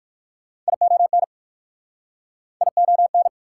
Morse code, heard as a keyed tone, about 60 wpm, 690 Hz. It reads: ION ION